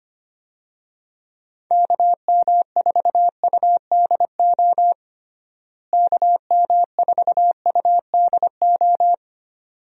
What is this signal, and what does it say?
Morse code, heard as a keyed tone, about 25 wpm, 700 Hz. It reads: KM4UDO KM4UDO
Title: USA Call Signs - 6 Characters - Repeated 2x - 25wpm